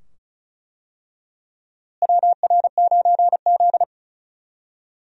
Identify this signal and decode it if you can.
Morse code, heard as a keyed tone, about 36 wpm, 700 Hz. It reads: WR9Z
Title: USA Call Signs - 4 Characters - Repeated 2x - 35wpm